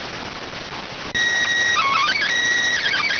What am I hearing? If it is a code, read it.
SSTV (slow-scan television), PD 290.